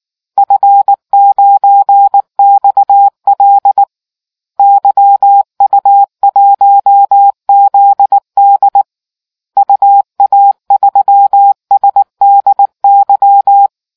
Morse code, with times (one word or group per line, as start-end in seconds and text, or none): 0.4-3.8 F9XL
4.6-8.8 YU1ZD
9.6-13.7 UA3SDY